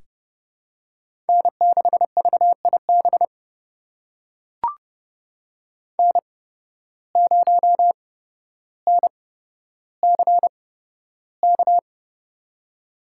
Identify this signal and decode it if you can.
Morse code, heard as a keyed tone, about 32 wpm, spaced as for 13 wpm, 700 Hz. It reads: N6VIB N0NCK
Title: USA Call Signs - 5 Characters - 8x Character Spacing - 30wpm